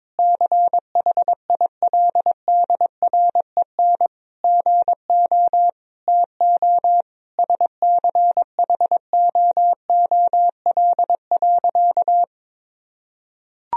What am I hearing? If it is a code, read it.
Morse code, heard as a keyed tone, about 22 wpm, 700 Hz. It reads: CHILDREN GO TO SCHOOL.
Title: Sentences from Top 1000 Words - Review Part 4 - 22wpm